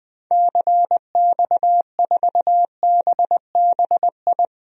Morse code, heard as a keyed tone, about 21 wpm, 700 Hz. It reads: CX4BBI
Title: DXCC-100 Call Signs - 6 Characters - Repeated 2x - 20wpm